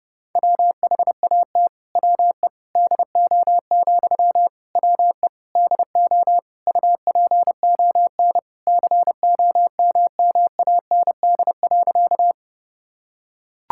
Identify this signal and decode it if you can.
Morse code, heard as a keyed tone, about 32 wpm, 700 Hz. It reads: WHAT WE DO, WE DO UPON COMMAND.
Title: Sentences from Top 1600 Words 30wpm